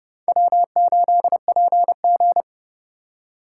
Morse code, heard as a keyed tone, about 38 wpm, 700 Hz. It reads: W8PG